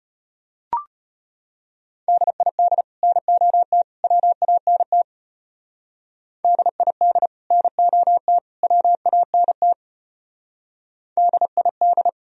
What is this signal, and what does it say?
Morse code, heard as a keyed tone, about 32 wpm, 700 Hz. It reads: DID NOT WANT DID NOT WANT DID